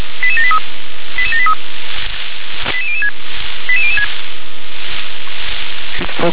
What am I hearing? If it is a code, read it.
Five-tone selective call (ZVEI), heard as quick runs of five tones, tones of about 70 ms, with speech. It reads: 88783 88783 88006 88006